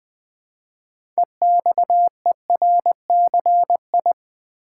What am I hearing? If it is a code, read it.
Morse code, heard as a keyed tone, about 20 wpm, 700 Hz. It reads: EXERCI
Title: Build A Word - 3 Letters At A Time - 8-Letter Words - 20wpm